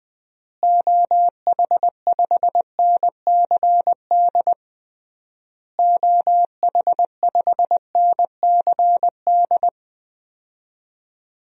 Morse code, heard as a keyed tone, about 20 wpm, 700 Hz. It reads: OH5NCD OH5NCD
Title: DXCC-100 Call Signs - 6 Characters - Repeated 3x - 20wpm